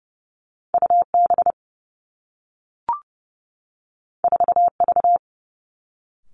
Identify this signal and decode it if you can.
Morse code, heard as a keyed tone, about 31 wpm, 700 Hz. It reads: U B 4 V
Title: Two Challenging Letter-Number Combinations 30wpm